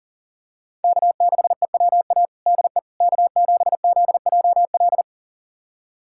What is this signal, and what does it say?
Morse code, heard as a keyed tone, about 37 wpm, 700 Hz. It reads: K6EWA DE K7ZJL
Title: USA Callsign DE Callsign - 5 Characters - 40wpm